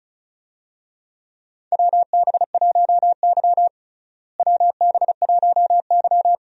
Morse code, heard as a keyed tone, about 36 wpm, 700 Hz. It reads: WB1Y WB1Y
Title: USA Call Signs - 4 Characters - Repeated 2x - 35wpm